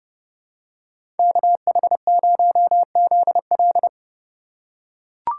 Morse code, heard as a keyed tone, about 29 wpm, 700 Hz. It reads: KH0ZL
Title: USA Call Signs - 5 Characters - 6x Character Spacing - 30wpm